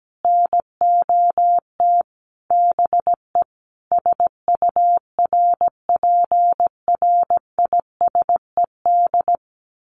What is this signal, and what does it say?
Morse code, heard as a keyed tone, about 17 wpm, 700 Hz. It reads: NOT BE SURPRISED
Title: Sets of 5 Words Encore2 17wpm